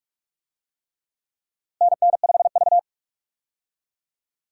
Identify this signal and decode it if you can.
Morse code, heard as a keyed tone, about 47 wpm, 700 Hz. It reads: NN5V
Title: USA Call Signs - 4 Characters - Repeated 3x - 45wpm